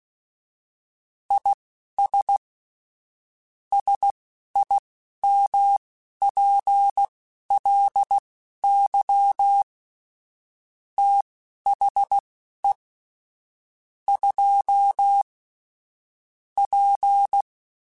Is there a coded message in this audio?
Yes: Morse code from 1.3 to 17.4 s, heard as a keyed tone, about 16 wpm, spaced as for 11 wpm, 800 Hz. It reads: IS SIMPLY THE 2 P